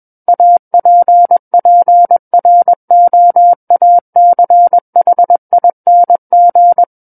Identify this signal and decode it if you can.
Morse code, heard as a keyed tone, about 21 wpm, 700 Hz. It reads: APPROACHING
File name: Book One / 12-Chapter 12 - 21 WPM